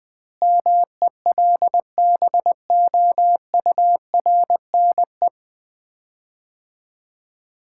Morse code, heard as a keyed tone, about 21 wpm, 700 Hz. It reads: MELBOURNE
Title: Common QSO Phrases - 2x Word-Spacing - 20wpm